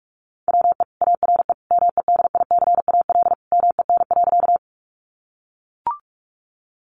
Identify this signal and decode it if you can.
Morse code, heard as a keyed tone, about 44 wpm, 700 Hz. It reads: WE ARE MEDICAL MEN.
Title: Sentences from Top 800 Words 45wpm